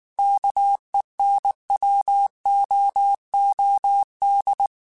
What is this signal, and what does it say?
Morse code, heard as a keyed tone, about 19 wpm, 800 Hz. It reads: KENWOOD
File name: QSOs / QSOs-mp3s / WD0BGP 1@18